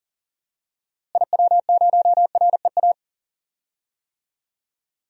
Morse code, heard as a keyed tone, about 40 wpm, 700 Hz. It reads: IW0REA
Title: DXCC-100 Call Signs - 6 Characters - Repeated 2x - 40wpm